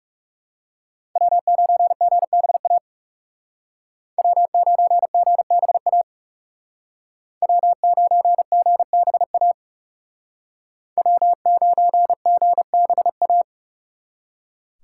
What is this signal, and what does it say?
Morse code, heard as a keyed tone, about 35 wpm, 700 Hz. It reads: W9GBA W9GBA W9GBA W9GBA
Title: Speed Racing - USA 1x3 Call Signs 30wpm